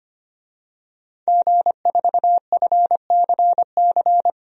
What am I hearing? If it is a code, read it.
Morse code, heard as a keyed tone, about 25 wpm, 700 Hz. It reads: G4FCC
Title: DXCC-100 Call Signs - 5 Characters - 25wpm